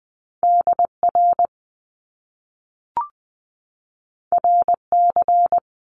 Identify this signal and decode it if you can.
Morse code, heard as a keyed tone, about 20 wpm, 700 Hz. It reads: DR RC
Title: Top Two Letter Combinations in Words 20wpm